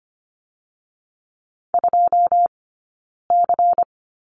Morse code, heard as a keyed tone, about 25 wpm, 700 Hz. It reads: 2 C
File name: course / 25:10wpm / 099 Lesson 17d - Letter u - Two Characters 25wpm